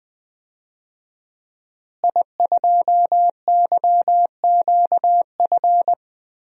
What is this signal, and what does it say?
Morse code, heard as a keyed tone, about 20 wpm, 700 Hz. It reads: I2YQF
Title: DXCC-100 Call Signs - 5 Characters - Repeated 3x - 20wpm